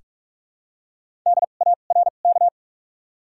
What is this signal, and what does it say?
Morse code, heard as a keyed tone, about 45 wpm, spaced as for 32 wpm, 700 Hz. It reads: DARK
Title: 2x Character Spacing - Top 500 Words - 45wpm